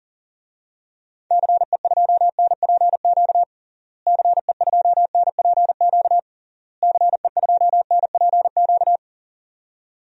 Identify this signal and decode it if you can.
Morse code, heard as a keyed tone, about 37 wpm, 700 Hz. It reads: CE2NPQ CE2NPQ CE2NPQ
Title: DXCC-100 Call Signs - 6 Characters - Repeated 3x - 40wpm